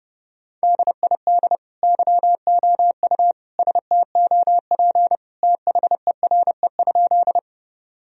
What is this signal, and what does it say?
Morse code, heard as a keyed tone, about 32 wpm, 700 Hz. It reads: DID YOU STOP THERE?
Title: Sentences from Top 1000 Words - Review Part 1 - 30wpm